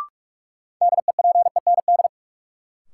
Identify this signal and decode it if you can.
Morse code, heard as a keyed tone, about 46 wpm, 700 Hz. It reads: DEPEND
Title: Sentences from Top 1300 Words - New Word Drill Alt - 45wpm